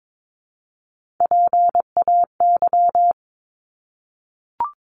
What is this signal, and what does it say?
Morse code, heard as a keyed tone, about 22 wpm, 700 Hz. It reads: PAY
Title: Top 501-750 Words 22wpm